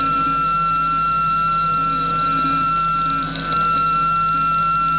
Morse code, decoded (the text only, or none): none